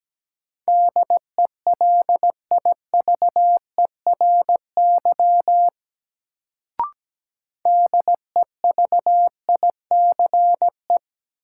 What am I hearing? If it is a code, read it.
Morse code, heard as a keyed tone, about 17 wpm, 700 Hz. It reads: DELIVERY DEVICE
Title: Sentences from Top 1500 Words - New Word Drill Alt - 17wpm